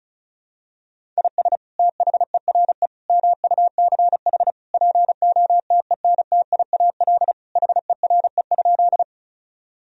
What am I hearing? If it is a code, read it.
Morse code, heard as a keyed tone, about 35 wpm, 700 Hz. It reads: IS THERE MUCH POTENTIAL HERE?
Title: Sentences from Top 1000 Words - Review Part 4 - 35wpm